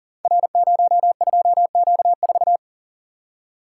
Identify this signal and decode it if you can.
Morse code, heard as a keyed tone, about 42 wpm, 690 Hz. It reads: R02Q4
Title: Code Groups - 5 Letter:Numbers - 40wpm